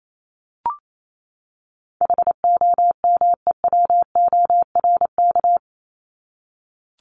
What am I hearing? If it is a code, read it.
Morse code, heard as a keyed tone, about 28 wpm, 700 Hz. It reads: HOMEWORK
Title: Top 1501-1750 Words 28wpm